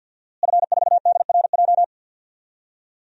Morse code, heard as a keyed tone, about 47 wpm, 700 Hz. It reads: F4DRJ
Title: DXCC-100 Call Signs - 5 Characters - Repeated 3x - 50wpm